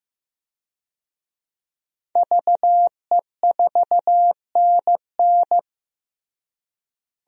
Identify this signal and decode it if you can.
Morse code, heard as a keyed tone, about 16 wpm, 700 Hz. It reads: VE4NN
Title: DXCC-100 Call Signs - 5 Characters - Repeated 2x - 15wpm